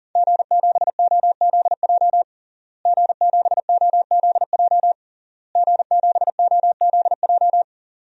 Morse code, heard as a keyed tone, about 37 wpm, 700 Hz. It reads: G7OZJ G7OZJ G7OZJ
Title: DXCC-100 Call Signs - 5 Characters - Repeated 3x - 40wpm